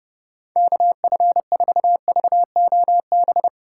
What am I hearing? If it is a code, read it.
Morse code, heard as a keyed tone, about 31 wpm, 700 Hz. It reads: KF4VOB
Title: USA Call Signs - 6 Characters - 4x Character Spacing - 30wpm